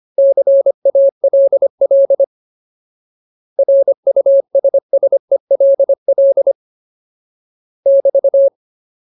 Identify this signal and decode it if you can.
Morse code, heard as a keyed tone, about 25 wpm, 550 Hz. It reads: CALL RUSSELL <BT>
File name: Ultimate Word List / 25WPM Ultimate